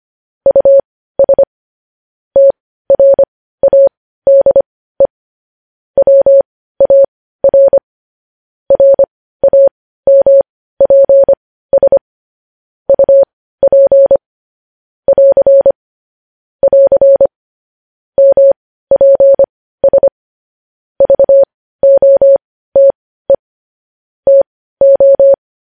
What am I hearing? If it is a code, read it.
Morse code, heard as a keyed tone, about 25 wpm, spaced as for 15 wpm, 550 Hz. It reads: US TRADE WAR RAMPS UP <AR> <AR> MPS VOTE TO
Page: news headlines in morse code at 15 wpm